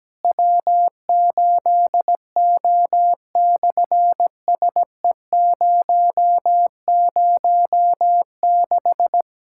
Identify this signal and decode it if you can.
Morse code, heard as a keyed tone, about 17 wpm, 700 Hz. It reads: W8O/SE006
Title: SOTA QSO Practice 17wpm